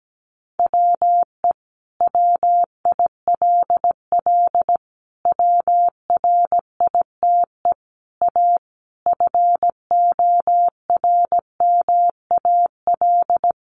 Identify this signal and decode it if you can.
Morse code, heard as a keyed tone, about 17 wpm, 700 Hz. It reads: WE WILL WRITE A FORMAL
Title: Sentences from Top 1300 Words 17wpm